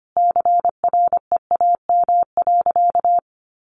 Morse code, heard as a keyed tone, about 25 wpm, 700 Hz. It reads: CREAM.
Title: Sentences from Top 1500 Words 25wpm